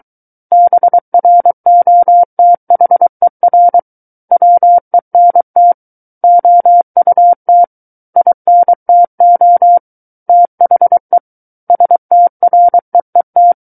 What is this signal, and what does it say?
Morse code, heard as a keyed tone, about 23 wpm, 700 Hz. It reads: BROTHER WENT OUT INTO THE STREET
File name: Book One / 14-Chapter 14 - 23 WPM